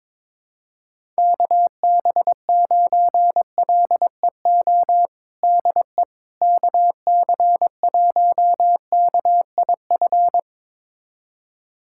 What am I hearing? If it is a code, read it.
Morse code, heard as a keyed tone, about 22 wpm, 700 Hz. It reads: KB9LEO DE KC1KIF
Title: USA Callsign DE Callsign - 6 Characters - 22wpm